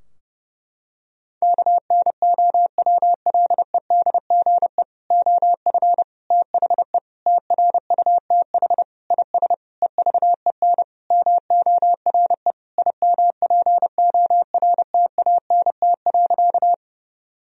Morse code, heard as a keyed tone, about 32 wpm, 700 Hz. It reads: KNOWLEDGE OF THE TRUTH IS EVEN MORE IMPORTANT.
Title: Sentences from Top 1000 Words - Review Part 3 - 30wpm